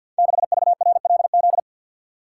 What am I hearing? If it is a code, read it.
Morse code, heard as a keyed tone, about 48 wpm, 700 Hz. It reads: 6VRLZ
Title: Code Groups - 5 Letter:Numbers - 50wpm